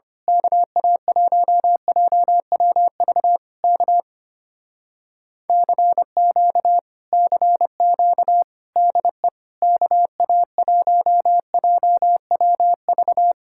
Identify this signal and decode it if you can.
Morse code, heard as a keyed tone, about 24 wpm, 700 Hz. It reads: KA1JWV K CQ CQ DE KA1JWV
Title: Speed-Racing Common QSO Phrases - 25wpm